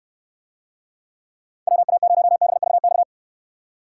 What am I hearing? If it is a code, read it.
Morse code, heard as a keyed tone, about 69 wpm, 700 Hz. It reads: WA0BVX